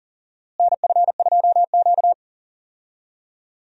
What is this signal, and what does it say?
Morse code, heard as a keyed tone, about 40 wpm, 700 Hz. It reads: NF2Q
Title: USA Call Signs - 4 Characters - Repeated 3x - 40wpm